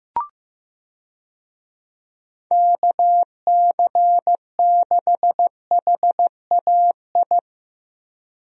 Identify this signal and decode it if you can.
Morse code, heard as a keyed tone, about 15 wpm, 700 Hz. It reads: KC6HAI
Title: USA Call Signs - 6 Characters - Repeated 3x - 15wpm